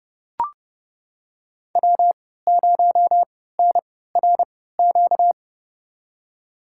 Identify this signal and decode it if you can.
Morse code, heard as a keyed tone, about 29 wpm, spaced as for 17 wpm, 700 Hz. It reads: W0NRQ
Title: USA Call Signs - 5 Characters - 3x Character Spacing - 30wpm